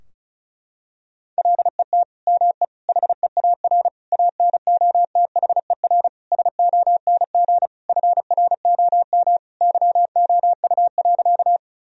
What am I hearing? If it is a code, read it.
Morse code, heard as a keyed tone, about 36 wpm, 700 Hz. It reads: LET ME HEAR ANOTHER SONG FROM YOU.
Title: Sentences from Top 1000 Words - Review Part 2 - 35wpm